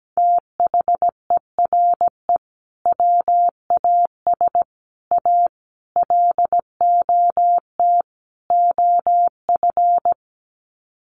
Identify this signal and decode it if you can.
Morse code, heard as a keyed tone, about 17 wpm, 700 Hz. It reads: THERE WAS A LOT OF
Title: Sets of 5 Words 17wpm